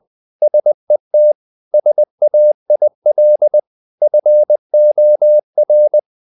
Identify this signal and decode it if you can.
Morse code, heard as a keyed tone, about 20 wpm, 600 Hz. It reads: SET SAIL FOR
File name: the old man and the sea part 2 20wpm